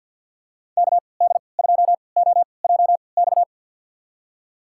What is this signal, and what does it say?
Morse code, heard as a keyed tone, about 50 wpm, spaced as for 28 wpm, 690 Hz. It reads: KD2OJX
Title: USA Call Signs - 3x Character Spacing - 50wpm